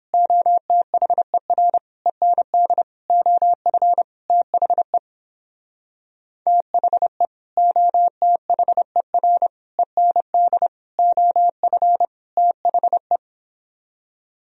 Morse code, heard as a keyed tone, about 29 wpm, 700 Hz. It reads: OTHER END OF THE THE OTHER END OF THE